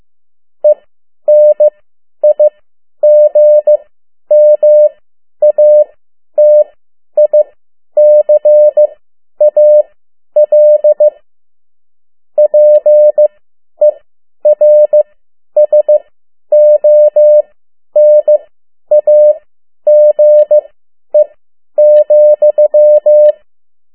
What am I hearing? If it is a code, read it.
Morse code, heard as a keyed tone, about 15 wpm, spaced as for 10 wpm, 600 Hz. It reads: ENIGMATICAL PERSONAGE,
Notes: Morse code 10 WPM